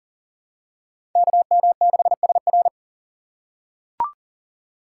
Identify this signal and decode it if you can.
Morse code, heard as a keyed tone, about 40 wpm, 700 Hz. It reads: KM6SR
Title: USA Call Signs - 5 Characters - 4x Character Spacing - 40wpm